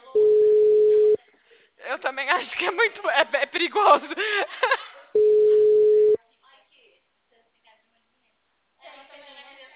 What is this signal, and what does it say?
Ringback tone (Europe, 425 Hz), with speech.
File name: out- -2003- - - .284